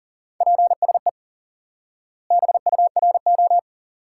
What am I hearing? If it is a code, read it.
Morse code, heard as a keyed tone, about 40 wpm, 700 Hz. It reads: PSE BURO
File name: IQE Course 3 - Lesson 89 - YAGI - Two Elements - 40wpm